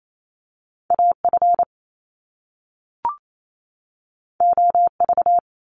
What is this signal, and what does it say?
Morse code, heard as a keyed tone, about 28 wpm, 700 Hz. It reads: AF OV
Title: Top Two Letter Combinations in Words 28wpm